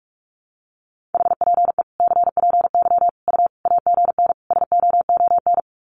Morse code, heard as a keyed tone, about 44 wpm, 700 Hz. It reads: HPE CPY U AGN SOON
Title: Common QSO Phrases 45wpm